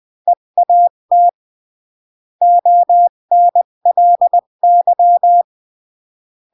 Morse code, heard as a keyed tone, about 20 wpm, 700 Hz. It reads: EAT ONLY
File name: CW / jodorowsky / 82 commandments 20-15